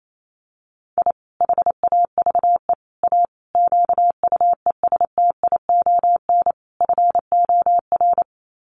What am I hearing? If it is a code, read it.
Morse code, heard as a keyed tone, about 28 wpm, 700 Hz. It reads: I HAVE A QUESTION FOR
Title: Sets of 5 Words Encore2 28wpm